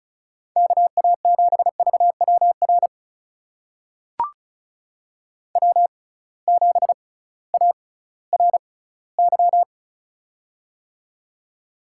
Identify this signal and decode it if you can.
Morse code, heard as a keyed tone, about 36 wpm, spaced as for 19 wpm, 700 Hz. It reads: KA7VWR W7ARY
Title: USA Call Signs - 6x Character Spacing - 35wpm